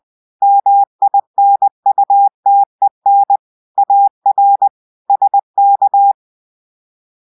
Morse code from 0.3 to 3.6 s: MINUTEN